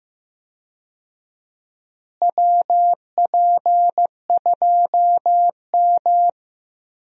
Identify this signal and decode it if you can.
Morse code, heard as a keyed tone, about 16 wpm, 700 Hz. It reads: WP2M
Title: USA Call Signs - 4 Characters - Repeated 3x - 15wpm